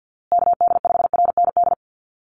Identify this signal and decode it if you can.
Morse code, heard as a keyed tone, about 50 wpm, 700 Hz. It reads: KD5RND
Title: USA Call Signs Repeated 2x 50wpm